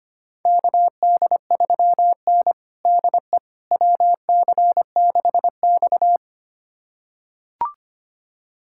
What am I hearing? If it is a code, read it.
Morse code, heard as a keyed tone, about 25 wpm, 700 Hz. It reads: KD3N DE WC6X
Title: USA Callsign DE Callsign - 4 Characters - 25wpm